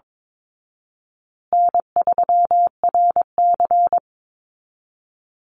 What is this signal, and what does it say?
Morse code, heard as a keyed tone, about 22 wpm, 700 Hz. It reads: N3RC
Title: Easy Call Signs Repeated 2x - 22wpm